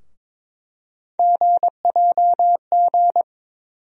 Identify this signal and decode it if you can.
Morse code, heard as a keyed tone, about 22 wpm, 700 Hz. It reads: GJG
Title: USA Call Sign Postfixes - 3 Letters 22wpm